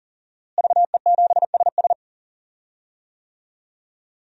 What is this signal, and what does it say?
Morse code, heard as a keyed tone, about 40 wpm, 700 Hz. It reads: VE7SS
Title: DXCC-100 Call Signs - 5 Characters - Repeated 3x - 40wpm